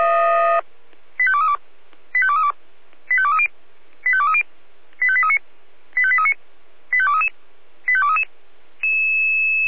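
Five-tone selective call (ZVEI), heard as quick runs of five tones, tones of about 70 ms. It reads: 85232 85232 85230 85230 85839 85839 85233 85233
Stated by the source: ZVEI five-tone selective call 85855 received on an SDR.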